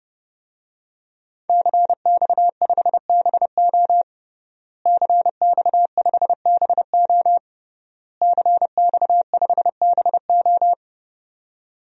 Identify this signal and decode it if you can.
Morse code, heard as a keyed tone, about 30 wpm, 700 Hz. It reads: CX5BO CX5BO CX5BO
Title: DXCC-100 Call Signs - 5 Characters - Repeated 3x - 30wpm